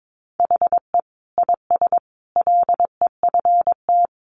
Morse code, heard as a keyed tone, about 22 wpm, 700 Hz. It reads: HE IS LEFT
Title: Sentences from Top 1100 Words 22wpm